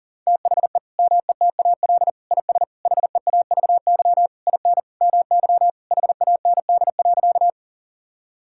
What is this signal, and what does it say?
Morse code, heard as a keyed tone, about 40 wpm, 700 Hz. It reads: THE METAL IS HEAVY IN MY HAND.
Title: Sentences from Top 1000 Words - Review Part 6 - 40wpm